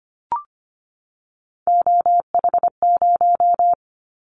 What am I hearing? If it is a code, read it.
Morse code, heard as a keyed tone, about 25 wpm, 700 Hz. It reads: OH0
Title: Callsign Prefixes for Top 100 DXCC - 25wpm